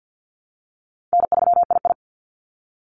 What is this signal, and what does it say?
Morse code, heard as a keyed tone, about 50 wpm, 700 Hz. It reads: N3II
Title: NA SOTA Activator Callsigns 50wpm